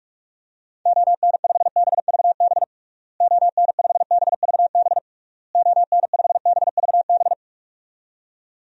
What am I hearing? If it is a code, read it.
Morse code, heard as a keyed tone, about 42 wpm, 700 Hz. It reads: ON5BVB ON5BVB ON5BVB
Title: DXCC-100 Call Signs - 6 Characters - Repeated 3x - 45wpm